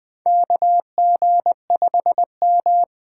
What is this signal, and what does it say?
Morse code, heard as a keyed tone, about 20 wpm, 700 Hz. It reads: KG5M
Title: USA Call Signs - 6 Characters - 8x Character Spacing - 20wpm